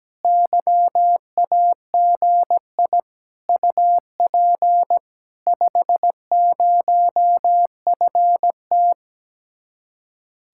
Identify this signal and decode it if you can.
Morse code, heard as a keyed tone, about 17 wpm, 700 Hz. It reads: YAGI UP 50FT
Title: Speed-Racing Common QSO Phrases - 17wpm